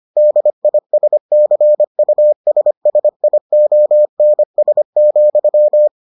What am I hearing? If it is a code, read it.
Morse code, heard as a keyed tone, about 25 wpm, 600 Hz. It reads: DISCUSSIONS,